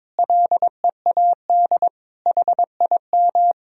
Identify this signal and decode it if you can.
Morse code, heard as a keyed tone, about 22 wpm, 700 Hz. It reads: LEAD HIM
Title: Sentences from Top 1000 Words - Review Part 2 - 22wpm